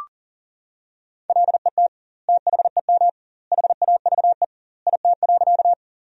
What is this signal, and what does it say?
Morse code, heard as a keyed tone, about 37 wpm, 700 Hz. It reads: LET THEM HAVE IT.
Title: 2x Word-Spacing - Sentences from Top 300 Words - 40wpm